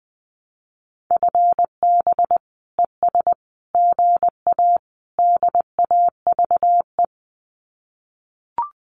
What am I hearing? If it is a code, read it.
Morse code, heard as a keyed tone, about 20 wpm, 700 Hz. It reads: FB ES GA DAVE
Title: Common QSO Phrases Encore 20wpm